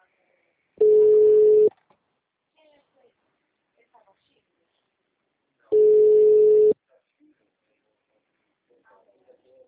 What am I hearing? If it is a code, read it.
Ringback tone (Europe, 425 Hz), with speech.
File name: out- -2026- - - .38923